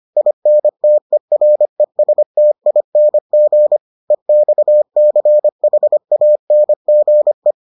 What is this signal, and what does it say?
Morse code, heard as a keyed tone, about 25 wpm, 600 Hz. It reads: INTERESTING EXCHANGE